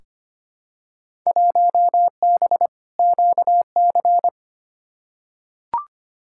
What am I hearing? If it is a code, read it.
Morse code, heard as a keyed tone, about 25 wpm, 700 Hz. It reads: 1B QC
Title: Field Day Exchanges - 25wpm